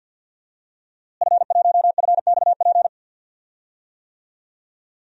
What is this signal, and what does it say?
Morse code, heard as a keyed tone, about 52 wpm, 700 Hz. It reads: F1FXP
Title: DXCC-100 Call Signs - 5 Characters - Repeated 3x - 50wpm